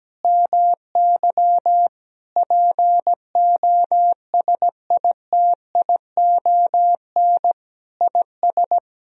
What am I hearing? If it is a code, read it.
Morse code, heard as a keyed tone, about 17 wpm, 700 Hz. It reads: MY POSITION IS
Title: Sentences from Top 1000 Words - Review Part 4 - 17wpm